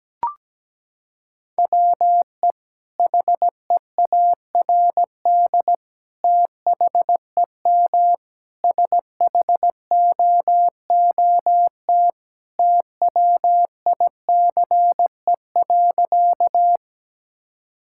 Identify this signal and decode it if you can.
Morse code, heard as a keyed tone, about 17 wpm, 700 Hz. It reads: WE HEARD THEM SHOOT TWICE.
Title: Sentences from Top 1600 Words 17wpm